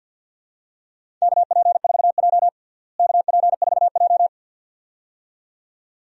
Morse code, heard as a keyed tone, about 52 wpm, 700 Hz. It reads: KP4J KP4J
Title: DXCC-100 Call Signs - 4 Characters - Repeated 2x - 50wpm